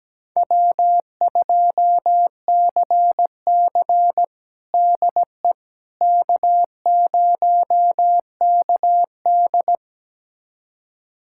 Morse code, heard as a keyed tone, about 17 wpm, 700 Hz. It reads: W2CC DE K0KD
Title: USA Callsign DE Callsign - 4 Characters - 17wpm